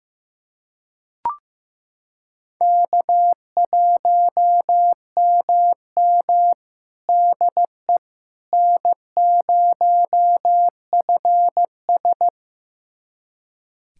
Morse code, heard as a keyed tone, about 16 wpm, 700 Hz. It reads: K1MM DE N0FS
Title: USA Callsign DE Callsign - 4 Characters - 15wpm